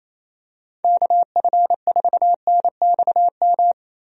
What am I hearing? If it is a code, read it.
Morse code, heard as a keyed tone, about 27 wpm, 700 Hz. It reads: KF4NXM
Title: USA Call Signs - 6 Characters - Repeated 2x - 28wpm